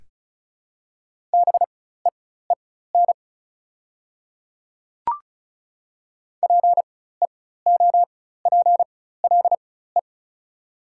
Morse code, heard as a keyed tone, about 34 wpm, spaced as for 16 wpm, 690 Hz. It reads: BEEN PEOPLE
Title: 4x Character Spacing - Top 100 Words - 35wpm